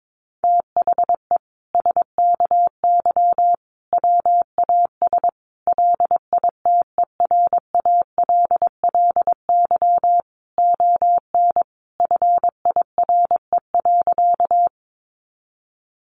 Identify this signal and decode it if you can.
Morse code, heard as a keyed tone, about 22 wpm, 700 Hz. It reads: THE SKY WAS LITERALLY ON FIRE.
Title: Sentences from Top 1400 Words 22wpm